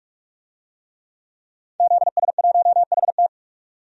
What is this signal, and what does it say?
Morse code, heard as a keyed tone, about 45 wpm, 700 Hz. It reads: ZS1HT
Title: DXCC-100 Call Signs - 5 Characters - 45wpm